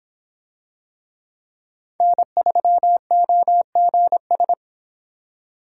Morse code, heard as a keyed tone, about 24 wpm, 700 Hz. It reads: N3OGS